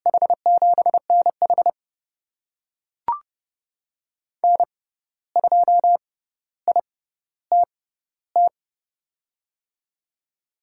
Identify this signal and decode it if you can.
Morse code, heard as a keyed tone, about 30 wpm, spaced as for 14 wpm, 700 Hz. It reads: H7NH N2ITT
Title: USA Call Signs - 5 Characters - 6x Character Spacing - 30wpm